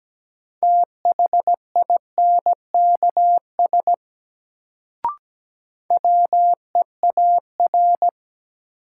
Morse code, heard as a keyed tone, about 17 wpm, 700 Hz. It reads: THINKS WEAR
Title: Sentences from Top 1000 Words - Review Part 6 - New Word Drill - 17wpm